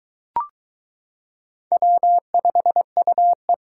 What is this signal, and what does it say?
Morse code, heard as a keyed tone, about 22 wpm, 700 Hz. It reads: W5UE